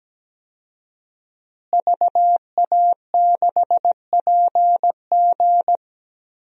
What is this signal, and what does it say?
Morse code, heard as a keyed tone, about 17 wpm, 700 Hz. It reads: VA6PG
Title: DXCC-100 Call Signs - 5 Characters - Repeated 2x - 17wpm